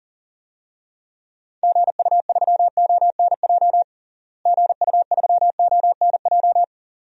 Morse code, heard as a keyed tone, about 40 wpm, 700 Hz. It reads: GU3ONJ GU3ONJ
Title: DXCC-100 Call Signs - 6 Characters - Repeated 2x - 40wpm